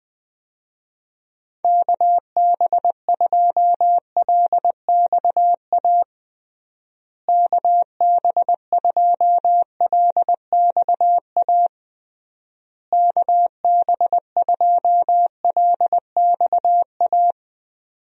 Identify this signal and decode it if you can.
Morse code, heard as a keyed tone, about 21 wpm, 700 Hz. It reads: KB2LXA KB2LXA KB2LXA
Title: USA Call Signs - 6 Characters - Repeated 3x - 20wpm